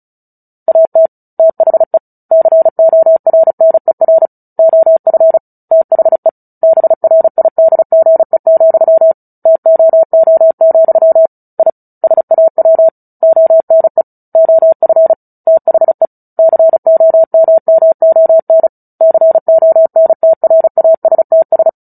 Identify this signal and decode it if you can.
Morse code, heard as a keyed tone, about 34 wpm, 650 Hz. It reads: AT THE CORNER OF THE BRIDGE, TOO, I SAW ONE OF THE COMMON CONTRASTS
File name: Book Two / 09-Chapter 9 - 35 WPM